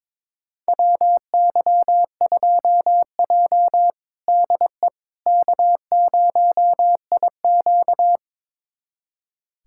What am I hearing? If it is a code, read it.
Morse code, heard as a keyed tone, about 22 wpm, 700 Hz. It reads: WY2J DE K0IQ
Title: USA Callsign DE Callsign - 4 Characters - 22wpm